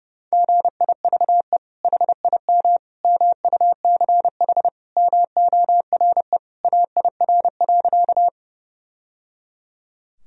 Morse code, heard as a keyed tone, about 32 wpm, 700 Hz. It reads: GIVE HIM MUCH MORE AIR.
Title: Sentences from Top 1000 Words - Review Part 1 - 30wpm